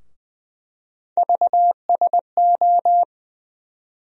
Morse code, heard as a keyed tone, about 20 wpm, 700 Hz. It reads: VSO
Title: USA Call Sign Postfixes - 3 Letters 20wpm